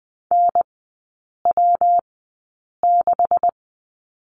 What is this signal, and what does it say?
Morse code, heard as a keyed tone, about 20 wpm, spaced as for 8 wpm, 690 Hz. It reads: NW6
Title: 159 Lesson 26e - Letter g - Three Characters 20wpm